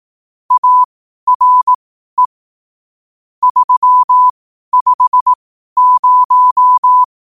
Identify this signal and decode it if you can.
Morse code, heard as a keyed tone, about 18 wpm, spaced as for 12 wpm, 1000 Hz. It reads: ARE 350
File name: qotdfiles / 12wpm